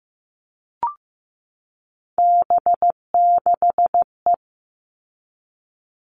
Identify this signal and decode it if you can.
Morse code, heard as a keyed tone, about 15 wpm, 700 Hz. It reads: B6E
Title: All Cut Numbers - 3-Digit 15wpm